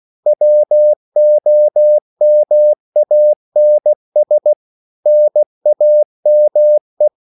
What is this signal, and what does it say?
Morse code, heard as a keyed tone, about 16 wpm, 600 Hz. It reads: WOMANS NAME